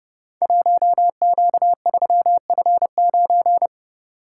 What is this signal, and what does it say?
Morse code, heard as a keyed tone, about 30 wpm, 700 Hz. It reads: 1Q3F9
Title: Code Groups - 5 Letter:Numbers - 30wpm